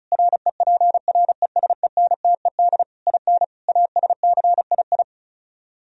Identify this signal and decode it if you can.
Morse code, heard as a keyed tone, about 43 wpm, 700 Hz. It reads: REPRESENTED IN ASCII
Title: Speed-Racing - Sets of 5 Words 35wpm